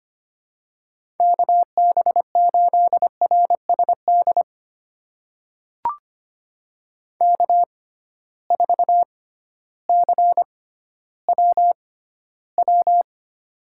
Morse code, heard as a keyed tone, about 25 wpm, spaced as for 13 wpm, 700 Hz. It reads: KB8RSD K4CWW
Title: USA Call Signs - 6x Character Spacing - 25wpm